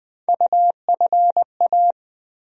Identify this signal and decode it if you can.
Morse code, heard as a keyed tone, about 20 wpm, 700 Hz. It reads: UFA
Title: USA Call Sign Postfixes - 3 Letters 20wpm